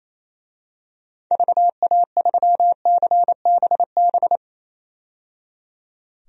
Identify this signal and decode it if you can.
Morse code, heard as a keyed tone, about 28 wpm, 700 Hz. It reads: VA3CBB
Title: DXCC-100 Call Signs - 6 Characters - 28wpm